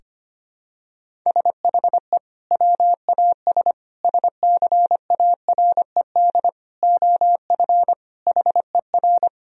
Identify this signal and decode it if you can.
Morse code, heard as a keyed tone, about 25 wpm, 700 Hz. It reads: SHE WAS SCARED OF HER
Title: Sentences from Top 1000 Words - Review Part 4 - 25wpm